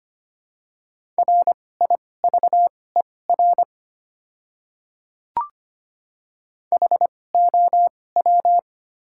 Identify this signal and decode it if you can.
Morse code, heard as a keyed tone, about 25 wpm, spaced as for 18 wpm, 700 Hz. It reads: RIVER HOW
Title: 2x Character Spacing - Top 300 Words - 25wpm